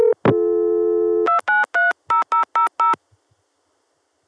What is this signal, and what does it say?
DTMF, heard as telephone touch tones, tones of about 130 ms. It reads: 2930000